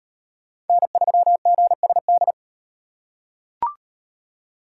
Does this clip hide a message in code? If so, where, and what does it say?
Yes: Morse code from 0.7 to 2.3 s, heard as a keyed tone, about 38 wpm, 700 Hz. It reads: N3GSD